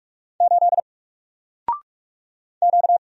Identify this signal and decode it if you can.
Morse code, heard as a keyed tone, about 46 wpm, 700 Hz. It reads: 8 Q
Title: Single Letter-Number Alt 45wpm